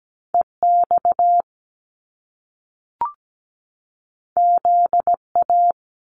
Morse code, heard as a keyed tone, about 17 wpm, 700 Hz. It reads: EX ZA
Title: Top Two Letter Combinations in Words 17wpm